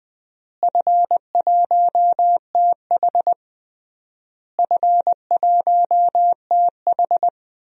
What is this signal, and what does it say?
Morse code, heard as a keyed tone, about 20 wpm, 700 Hz. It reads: F1TH F1TH
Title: DXCC-100 Call Signs - 4 Characters - Repeated 3x - 20wpm